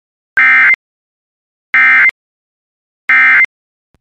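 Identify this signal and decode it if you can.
Emergency Alert System (SAME) tones, heard as harsh buzzing bursts. It is heard from 0.3 to 3.4 s.
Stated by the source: Emergency alert tones from a Weatherscan IntelliStar weather channel system.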